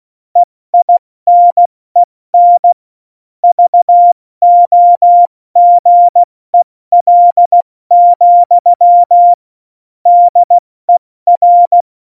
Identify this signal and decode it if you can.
Morse code, heard as a keyed tone, about 16 wpm, 700 Hz. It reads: EINEN VOGEL, DER
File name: Morsekurs / Witze / Witz 19. November 2019, 70 F80